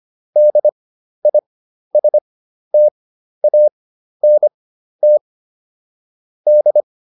Morse code, heard as a keyed tone, about 25 wpm, spaced as for 12 wpm, 600 Hz. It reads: DISTANT D